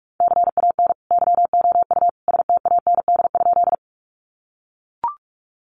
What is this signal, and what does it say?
Morse code, heard as a keyed tone, about 44 wpm, 700 Hz. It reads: CAN YOU STAND?
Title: Sentences from Top 300 Words 45wpm